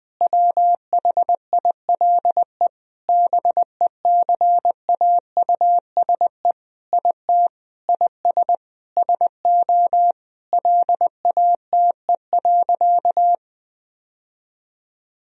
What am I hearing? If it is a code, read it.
Morse code, heard as a keyed tone, about 20 wpm, 700 Hz. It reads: WHILE BECAUSE IT IS SO LATE.
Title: Sentences from Top 1000 Words - Review Part 3 - 20wpm